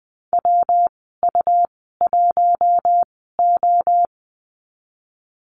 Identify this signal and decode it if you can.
Morse code, heard as a keyed tone, about 20 wpm, spaced as for 15 wpm, 700 Hz. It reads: WU1O
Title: 122 Lesson 20g - Number 3 - Call Signs 20wpm